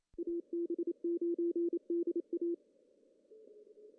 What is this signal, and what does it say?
Morse code, heard as a keyed tone, about 27 wpm, 380 Hz. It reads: AB9DA
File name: CW / cqww-cw-k5tr